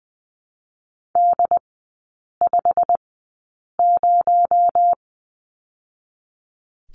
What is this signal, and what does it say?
Morse code, heard as a keyed tone, about 20 wpm, 700 Hz. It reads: D 5 0
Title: 218 Lesson 35e - Number 0 - Three Characters 20wpm